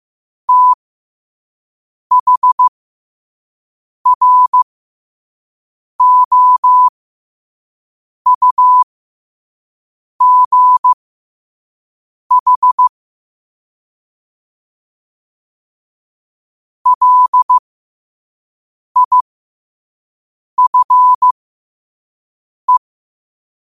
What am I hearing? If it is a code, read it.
Morse code, heard as a keyed tone, about 15 wpm, spaced as for 5 wpm, 1000 Hz. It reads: THROUGH LIFE
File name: qotdfiles / 5wpm